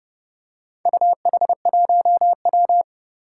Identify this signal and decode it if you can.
Morse code, heard as a keyed tone, about 31 wpm, 700 Hz. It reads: UH1W
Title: Four Challenging Letter-Number Combinations 30wpm